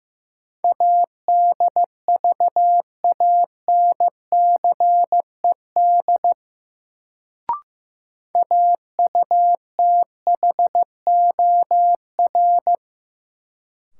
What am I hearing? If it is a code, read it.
Morse code, heard as a keyed tone, about 15 wpm, 700 Hz. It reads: ADVANCED AUTHOR
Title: Sentences from Top 1200 Words - New Word Drill - 15wpm